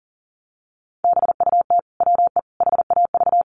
Morse code, heard as a keyed tone, about 40 wpm, 700 Hz. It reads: BUT WE HAV
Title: Sentences from Top 1300 Words 40wpm